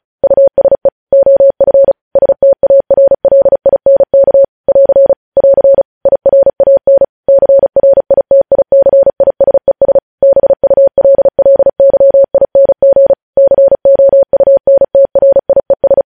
Morse code, heard as a keyed tone, about 36 wpm, 550 Hz. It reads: USE OF STARLINK <AR> <AR> IRAN CRITICISES BULLYING COUNTRIES
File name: podcast / News headlines-1x-35x35